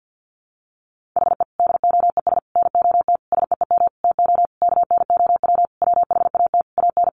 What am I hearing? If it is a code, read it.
Morse code, heard as a keyed tone, about 50 wpm, 700 Hz. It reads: HE DOES NOT SEEM TO KNOW WHAT AN
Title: Sentences from Top 1400 Words 50wpm